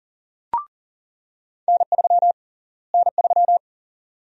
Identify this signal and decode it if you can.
Morse code, heard as a keyed tone, about 40 wpm, 700 Hz. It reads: N3 N3
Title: USA Call Sign Prefixes - Repeated 2x - 40wpm